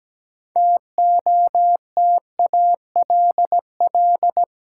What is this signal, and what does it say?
Morse code, heard as a keyed tone, about 17 wpm, 700 Hz. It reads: TOTALL
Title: Seven Letter Words - 17wpm